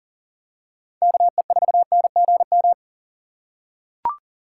Morse code, heard as a keyed tone, about 40 wpm, 700 Hz. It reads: KE4NGM
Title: USA Call Signs - 6 Characters - 4x Character Spacing - 40wpm